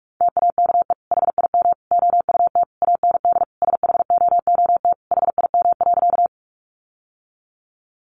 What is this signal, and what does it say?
Morse code, heard as a keyed tone, about 44 wpm, 700 Hz. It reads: TAKE HIM OUT AND SHOOT HIM.
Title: Sentences from Top 1200 Words 45wpm